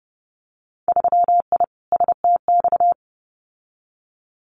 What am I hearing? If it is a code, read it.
Morse code, heard as a keyed tone, about 30 wpm, 700 Hz. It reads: 3I STX
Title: Winter Field Day Exchanges 30wpm